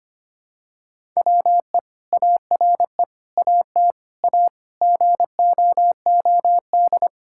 Morse code, heard as a keyed tone, about 25 wpm, 700 Hz. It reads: WE ARE AT A GOOD
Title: Sentences from Top 1000 Words - Review Part 6 - 25wpm